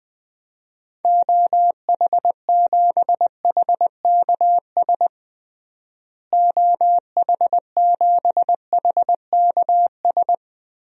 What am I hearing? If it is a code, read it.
Morse code, heard as a keyed tone, about 20 wpm, 700 Hz. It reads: OH7HKS OH7HKS
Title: DXCC-100 Call Signs - 6 Characters - Repeated 3x - 20wpm